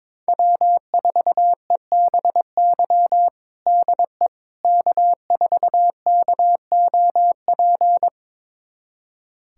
Morse code, heard as a keyed tone, about 22 wpm, 700 Hz. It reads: W4EBY DE K4KOP
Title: USA Callsign DE Callsign - 5 Characters - 22wpm